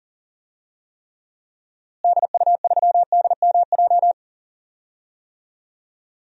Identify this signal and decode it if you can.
Morse code, heard as a keyed tone, about 40 wpm, 700 Hz. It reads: DU3DMJ
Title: DXCC-100 Call Signs - 6 Characters - Repeated 2x - 40wpm